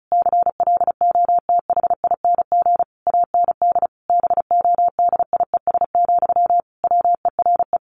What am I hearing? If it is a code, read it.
Morse code, heard as a keyed tone, about 34 wpm, 700 Hz. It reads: CLOTHING AND BODIES, WERE
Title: A Sound Of Thunder 35wpm